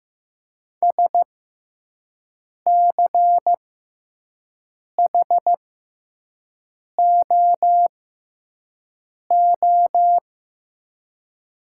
Morse code, heard as a keyed tone, about 15 wpm, spaced as for 5 wpm, 700 Hz. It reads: SCHOO
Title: 6x Character Spacing - Top 500 Words - 15wpm